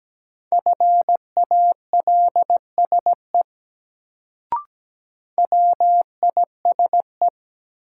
Morse code, heard as a keyed tone, about 17 wpm, 700 Hz. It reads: FALSE WISE
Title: Sentences from Top 1600 Words - New Word Drill Alt - 17wpm